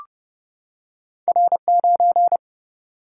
Morse code, heard as a keyed tone, about 30 wpm, 700 Hz. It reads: R9
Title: Single-Letter Single-Number 30wpm